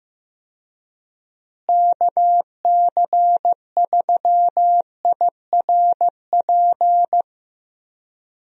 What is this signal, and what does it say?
Morse code, heard as a keyed tone, about 15 wpm, 700 Hz. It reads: KC3IRP
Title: USA Call Signs - 6 Characters - Repeated 3x - 15wpm